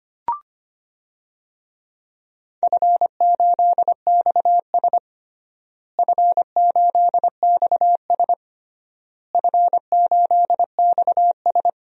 Morse code, heard as a keyed tone, about 25 wpm, 700 Hz. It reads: F8XS F8XS F8XS
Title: DXCC-100 Call Signs - 4 Characters - Repeated 3x - 25wpm